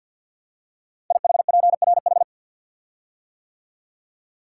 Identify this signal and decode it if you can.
Morse code, heard as a keyed tone, about 52 wpm, 690 Hz. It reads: IHPRH
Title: Code Groups - 5 Letters - 50wpm